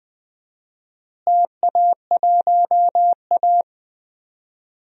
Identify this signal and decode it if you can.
Morse code, heard as a keyed tone, about 20 wpm, 700 Hz. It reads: TA1A
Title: DXCC-100 Call Signs - 4 Characters - Repeated 2x - 20wpm